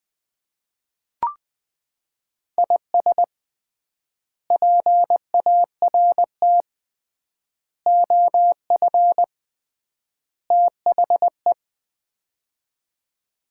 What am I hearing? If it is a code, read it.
Morse code, heard as a keyed tone, about 20 wpm, 700 Hz. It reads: IS PART OF THE
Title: 3x Word-Spacing - Sets of 4 Words - 20wpm